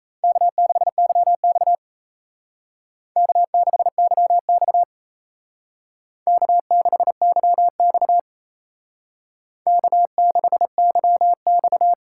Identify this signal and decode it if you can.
Morse code, heard as a keyed tone, about 37 wpm, 700 Hz. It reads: K6YX K6YX K6YX K6YX